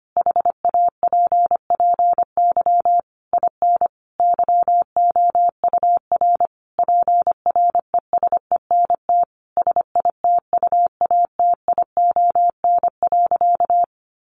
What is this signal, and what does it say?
Morse code, heard as a keyed tone, about 25 wpm, 700 Hz. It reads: HAPPY IN YOUR PRESENT SITUATION.
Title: Sentences from Top 1200 Words 25wpm